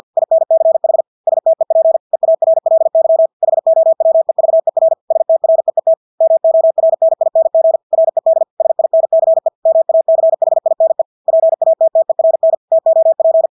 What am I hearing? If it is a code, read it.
Morse code, heard as a keyed tone, about 50 wpm, 650 Hz. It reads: INCH STEP EARLY HOWEVER STREET MORNING RED SINCE MACHINE PATTERN TOP
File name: Engelske ord 50 wpm